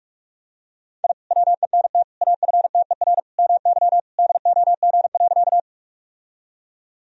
Morse code, heard as a keyed tone, about 46 wpm, 700 Hz. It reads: I WENT AFTER MY DOG.
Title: Sentences from Top 1000 Words - Review Part 2 - 45wpm